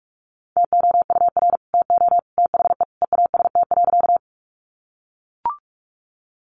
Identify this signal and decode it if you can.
Morse code, heard as a keyed tone, about 44 wpm, 700 Hz. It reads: TOUR TO THE EAST.
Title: Sentences from Top 1400 Words 45wpm